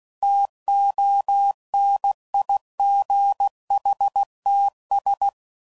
Morse code, heard as a keyed tone, about 16 wpm, 800 Hz. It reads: TONIGHTS
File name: CW Code Groups @15WPM / Weather Sets / Weather 5 5